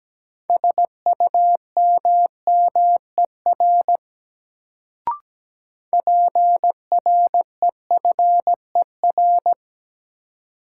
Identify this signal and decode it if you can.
Morse code, heard as a keyed tone, about 17 wpm, 700 Hz. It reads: SUMMER PREFER
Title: Sentences from Top 1100 Words - New Word Drill Alt - 17wpm